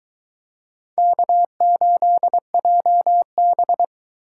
Morse code, heard as a keyed tone, about 24 wpm, 700 Hz. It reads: K8JB